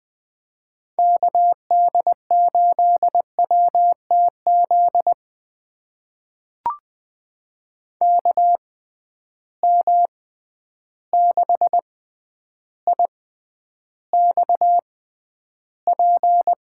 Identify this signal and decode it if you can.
Morse code, heard as a keyed tone, about 21 wpm, spaced as for 10 wpm, 700 Hz. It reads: KD8WTZ KM6IXP
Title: USA Call Signs - 6 Characters - 6x Character Spacing - 20wpm